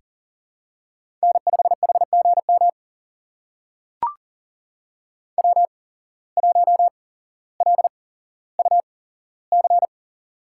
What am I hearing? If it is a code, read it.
Morse code, heard as a keyed tone, about 40 wpm, spaced as for 17 wpm, 700 Hz. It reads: N5HGM W1LUC
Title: USA Call Signs - 5 Characters - 8x Character Spacing - 40wpm